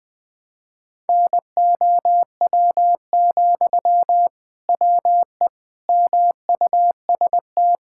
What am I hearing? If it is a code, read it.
Morse code, heard as a keyed tone, about 21 wpm, 700 Hz. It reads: NOW, WE MUST
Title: Sentences from Top 1000 Words - Review Part 6 - 20wpm